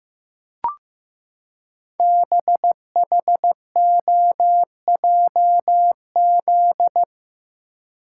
Morse code, heard as a keyed tone, about 15 wpm, 700 Hz. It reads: BHOJZ
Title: Code Groups - 5 Letters - 15wpm